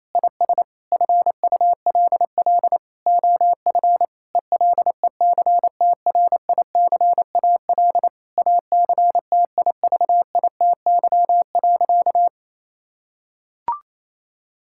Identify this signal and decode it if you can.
Morse code, heard as a keyed tone, about 28 wpm, 700 Hz. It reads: IS FULL OF ELECTRICAL ACTIVITY.
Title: Sentences from Top 1000 Words - Review Part 6 - 28wpm